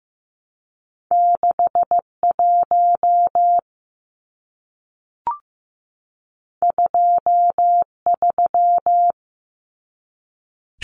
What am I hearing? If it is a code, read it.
Morse code, heard as a keyed tone, about 15 wpm, 700 Hz. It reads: 61 23
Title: Numbers - 2-Digit 15wpm